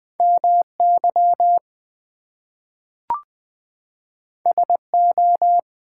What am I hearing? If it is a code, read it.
Morse code, heard as a keyed tone, about 20 wpm, 700 Hz. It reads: MY SO
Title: Two Letter Words - 20wpm